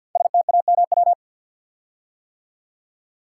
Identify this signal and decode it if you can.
Morse code, heard as a keyed tone, about 48 wpm, 700 Hz. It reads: STAMW
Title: Code Groups - 5 Letters - 50wpm